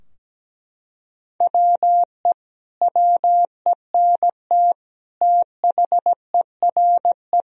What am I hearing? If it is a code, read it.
Morse code, heard as a keyed tone, about 17 wpm, 700 Hz. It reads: WE WENT THERE
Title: Sentences from Top 1000 Words - Review Part 4 - 17wpm